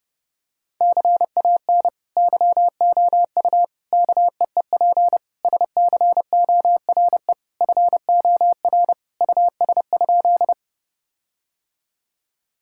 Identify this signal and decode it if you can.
Morse code, heard as a keyed tone, about 30 wpm, 700 Hz. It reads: CAN YOU KEEP SCORE FOR US?
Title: Sentences from Top 1600 Words 30wpm